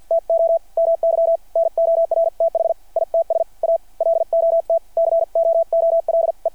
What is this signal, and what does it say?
Morse code, heard as a keyed tone, about 53 wpm, spaced as for 46 wpm, 660 Hz. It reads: TO MY NORTH ITS A LOT COOLE
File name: archived / CRSnet